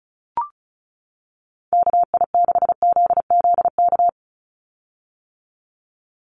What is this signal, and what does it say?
Morse code, heard as a keyed tone, about 34 wpm, 700 Hz. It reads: KI6ZZK
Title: USA Call Signs 35wpm